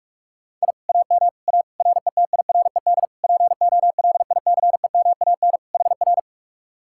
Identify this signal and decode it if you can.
Morse code, heard as a keyed tone, about 59 wpm, spaced as for 52 wpm, 700 Hz. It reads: I AM A RETIRED POLICEMAN HR
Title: Speed-Racing Common QSO Phrases - 45wpm